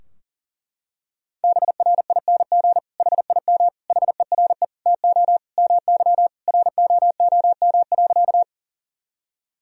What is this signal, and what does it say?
Morse code, heard as a keyed tone, about 40 wpm, 700 Hz. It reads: BRING HIM HERE TO MY ROOM.
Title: Sentences from Top 1000 Words - Review Part 3 - 40wpm